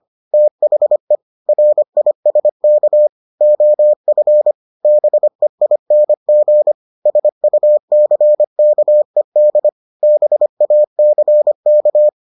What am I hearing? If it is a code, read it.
Morse code, heard as a keyed tone, about 25 wpm, 600 Hz. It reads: THE RISK OF BEING SUCKED BACK